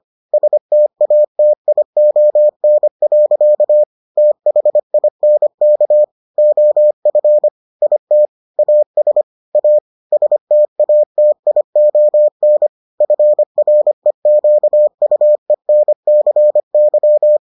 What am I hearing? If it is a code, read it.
Morse code, heard as a keyed tone, about 25 wpm, 600 Hz. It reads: STATION. THINK OF IT AS A STATION FREQUENCY